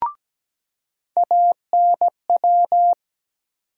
Morse code, heard as a keyed tone, about 17 wpm, 700 Hz. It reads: ANW
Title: USA Call Sign Postfixes - 3 Letters 17wpm